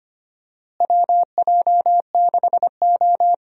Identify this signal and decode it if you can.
Morse code, heard as a keyed tone, about 25 wpm, 700 Hz. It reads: WJ6O
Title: USA Call Signs - 4 Characters - Repeated 2x - 25wpm